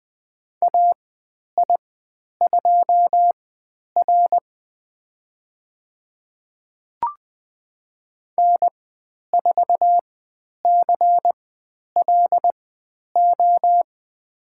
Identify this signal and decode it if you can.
Morse code, heard as a keyed tone, about 20 wpm, spaced as for 10 wpm, 700 Hz. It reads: AI2R N4CLO
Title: 095 Lesson 16g - Letter c - Call Signs 20wpm